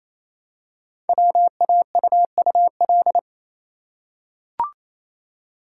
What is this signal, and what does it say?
Morse code, heard as a keyed tone, about 27 wpm, 700 Hz. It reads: WAUUL
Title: Code Groups - 5 Letters - 28wpm